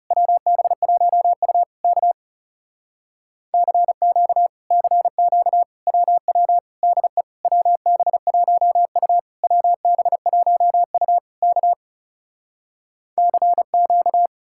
Morse code, heard as a keyed tone, about 37 wpm, 700 Hz. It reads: WB1U K CQ CQ WW DE WB1U WB1U K CQ
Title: Speed-Racing Common QSO Phrases - 30wpm